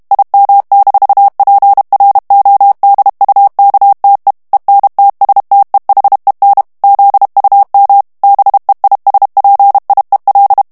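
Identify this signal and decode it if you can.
Morse code, heard as a keyed tone, about 32 wpm, 800 Hz. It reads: IM-PRODUKTE ENTSTEHEN ZUM BEISPIEL
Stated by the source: German plain-text Morse reading practice at 30 words per minute.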